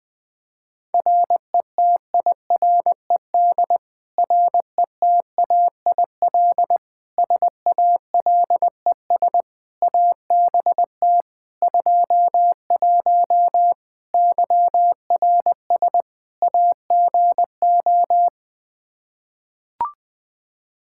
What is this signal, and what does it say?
Morse code, heard as a keyed tone, about 20 wpm, 700 Hz. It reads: RETIRED RETAIL SALES ABT 21 YRS AGO
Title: Practice QSO - Exchange 3 - More Info - 20wpm